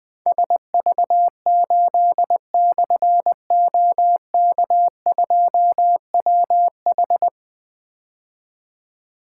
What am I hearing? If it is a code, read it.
Morse code, heard as a keyed tone, about 21 wpm, 700 Hz. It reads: SV8/OK2WH
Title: Difficult Call Signs 20wpm